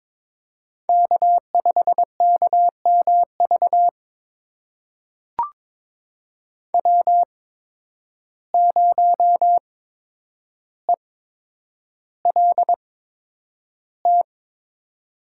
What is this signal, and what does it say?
Morse code, heard as a keyed tone, about 22 wpm, spaced as for 9 wpm, 700 Hz. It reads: K5KMV W0ELT
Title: USA Call Signs - 5 Characters - 8x Character Spacing - 22wpm